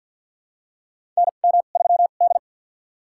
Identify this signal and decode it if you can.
Morse code, heard as a keyed tone, about 47 wpm, spaced as for 35 wpm, 700 Hz. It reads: NM3D
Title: USA Call Signs - 4 Characters - 2x Character Spacing - 50wpm